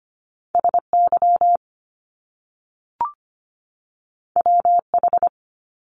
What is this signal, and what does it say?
Morse code, heard as a keyed tone, about 25 wpm, 700 Hz. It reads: SY WH
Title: Top Two Letter Combinations in Words 25wpm